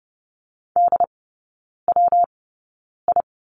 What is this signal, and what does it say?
Morse code, heard as a keyed tone, about 30 wpm, spaced as for 9 wpm, 700 Hz. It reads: DWI
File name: Lesson 19e - Letter w - Three Characters 30wpm